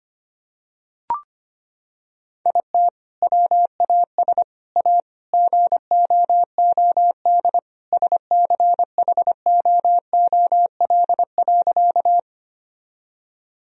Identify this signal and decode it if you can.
Morse code, heard as a keyed tone, about 25 wpm, 700 Hz. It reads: IT WAS A GOOD SCHOOL.
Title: Sentences from Top 1000 Words - Review Part 1 - 25wpm